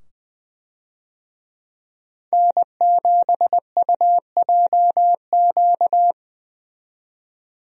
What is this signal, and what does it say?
Morse code, heard as a keyed tone, about 20 wpm, 700 Hz. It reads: N7UJQ
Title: USA Call Signs - 5 Characters - 20wpm